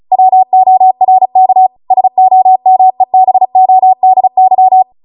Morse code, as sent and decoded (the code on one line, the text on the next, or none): .-- --- .-. -.- / ... --- -- . -... --- -.. -.--
WORK SOMEBODY